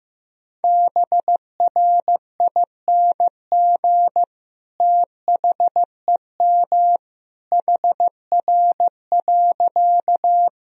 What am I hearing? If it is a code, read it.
Morse code, heard as a keyed tone, about 15 wpm, 700 Hz. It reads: BRING THEM HR.
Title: Sentences from Top 500 Words and Abbreviations 15wpm